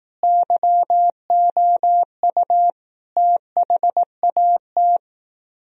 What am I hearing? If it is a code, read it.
Morse code, heard as a keyed tone, about 18 wpm, 700 Hz. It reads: YOU THAT
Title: Speed-Racing - Sets of 5 Words 15wpm